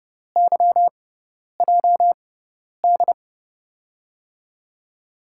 Morse code, heard as a keyed tone, about 32 wpm, spaced as for 10 wpm, 700 Hz. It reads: YJD
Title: USA Call Signs - 6x Character Spacing - 30wpm